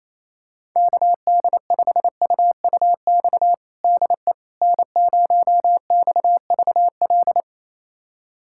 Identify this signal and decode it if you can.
Morse code, heard as a keyed tone, about 28 wpm, 700 Hz. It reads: KD5UUX DE N0XVL
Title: USA Callsign DE Callsign 28wpm